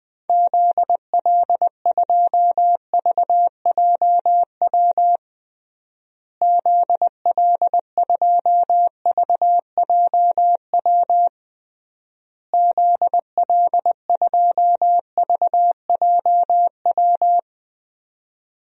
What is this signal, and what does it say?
Morse code, heard as a keyed tone, about 20 wpm, 700 Hz. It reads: ZL2VJW ZL2VJW ZL2VJW
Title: DXCC-100 Call Signs - 6 Characters - Repeated 3x - 20wpm